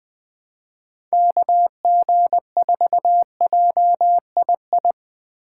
Morse code, heard as a keyed tone, about 20 wpm, 700 Hz. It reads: KG4JII
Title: USA Call Signs - 6 Characters - 2x Character Spacing - 20wpm